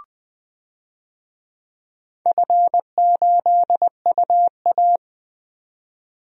Morse code, heard as a keyed tone, about 20 wpm, 700 Hz. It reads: F8UA
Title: DXCC-100 Call Signs - 4 Characters - Repeated 3x - 20wpm